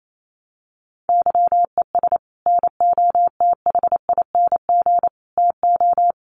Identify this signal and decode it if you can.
Morse code, heard as a keyed tone, about 28 wpm, 700 Hz. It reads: YES NOTHING TO
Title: Sentences from Top 1300 Words 28wpm